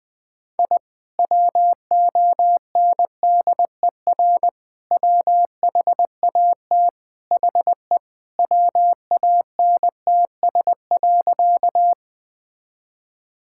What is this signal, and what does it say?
Morse code, heard as a keyed tone, about 20 wpm, 700 Hz. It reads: I WONDER WHAT HE WANTS.
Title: Sentences with 2-Word Phrases - 20wpm